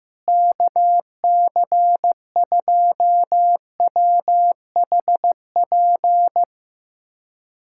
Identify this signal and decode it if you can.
Morse code, heard as a keyed tone, about 15 wpm, 700 Hz. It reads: KC2WHP
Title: USA Callsign DE Callsign - 6 Characters - 15wpm